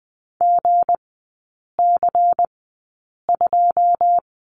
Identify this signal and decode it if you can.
Morse code, heard as a keyed tone, about 20 wpm, spaced as for 8 wpm, 700 Hz. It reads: GC2
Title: 159 Lesson 26e - Letter g - Three Characters 20wpm